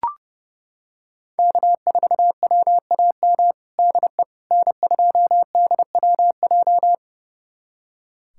Morse code, heard as a keyed tone, about 30 wpm, 700 Hz. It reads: K4WAM DE N2DWJ
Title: USA Callsign DE Callsign - 5 Characters - 30wpm